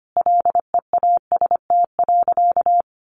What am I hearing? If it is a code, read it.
Morse code, heard as a keyed tone, about 25 wpm, 700 Hz. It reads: LEAST.
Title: Sentences from Top 900 Words 25wpm